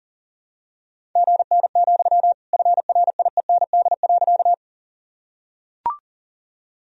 Morse code, heard as a keyed tone, about 40 wpm, 700 Hz. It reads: GN, FRIEND.
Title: Sentences from Top 300 Words and Abbreviations 40wpm